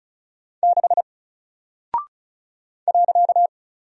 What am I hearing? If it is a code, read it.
Morse code, heard as a keyed tone, about 36 wpm, 700 Hz. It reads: 6 .
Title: Single Character (Letter:Number:Punctuation:Prosign) Alt 35wpm